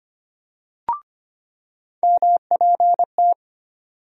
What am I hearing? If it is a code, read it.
Morse code, heard as a keyed tone, about 25 wpm, 700 Hz. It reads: MPT
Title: USA Call Sign Postfixes - 3 Letters 25wpm